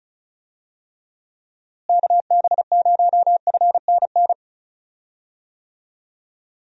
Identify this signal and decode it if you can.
Morse code, heard as a keyed tone, about 36 wpm, 700 Hz. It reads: KB0FNN
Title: USA Call Signs - 6 Characters - Repeated 2x - 35wpm